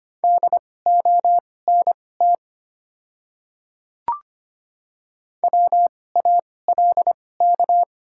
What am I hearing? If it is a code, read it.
Morse code, heard as a keyed tone, about 25 wpm, spaced as for 18 wpm, 700 Hz. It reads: DONT WALK
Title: 2x Character Spacing - Top 300 Words - 25wpm